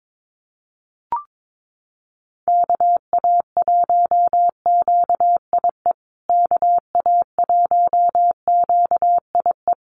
Morse code, heard as a keyed tone, about 22 wpm, 700 Hz. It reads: KA1QIE KA1QIE
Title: USA Call Signs Repeated 3x 22wpm